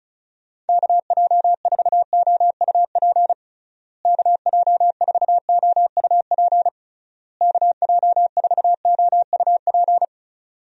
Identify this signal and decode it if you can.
Morse code, heard as a keyed tone, about 34 wpm, 700 Hz. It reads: KJ4OUP KJ4OUP KJ4OUP
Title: USA Call Signs - 6 Characters - Repeated 3x - 35wpm